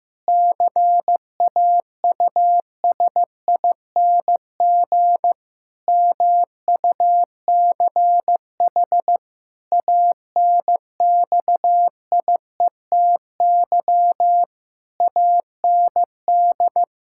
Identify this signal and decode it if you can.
Morse code, heard as a keyed tone, about 15 wpm, 700 Hz. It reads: CAUSING MUCH ANXIETY AND
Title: Pangram Challenge v2 - 15wpm